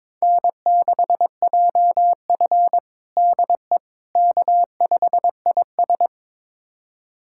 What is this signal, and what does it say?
Morse code, heard as a keyed tone, about 22 wpm, 700 Hz. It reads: N6JF DE K5IS
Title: USA Callsign DE Callsign - 4 Characters - 22wpm